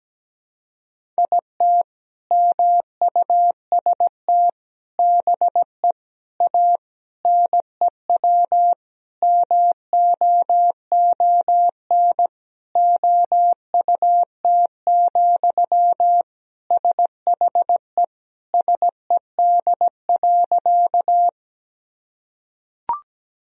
Morse code, heard as a keyed tone, about 17 wpm, 700 Hz. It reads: IT MUST BE A NEW MOON OUT, SHE SED.
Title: Sentences from Top 500 Words and Abbreviations 17wpm